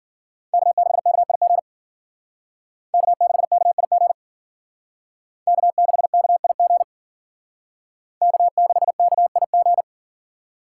Morse code, heard as a keyed tone, about 51 wpm, 700 Hz. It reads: K6KIG K6KIG K6KIG K6KIG